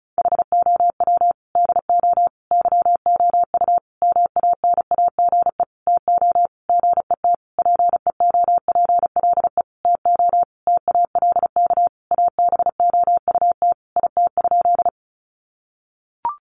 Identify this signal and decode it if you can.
Morse code, heard as a keyed tone, about 34 wpm, 700 Hz. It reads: HOW DO YOU MANAGE TO GET PEOPLE TO TALK ABOUT IT?
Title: Sentences from Top 1400 Words 35wpm